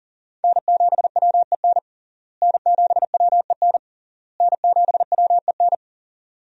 Morse code, heard as a keyed tone, about 37 wpm, 700 Hz. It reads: N7WEN N7WEN N7WEN
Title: USA Call Signs - 5 Characters - Repeated 3x - 40wpm